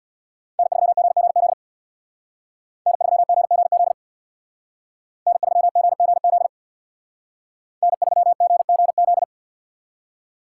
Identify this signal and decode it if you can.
Morse code, heard as a keyed tone, about 63 wpm, 700 Hz. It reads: N3GGZ N3GGZ N3GGZ N3GGZ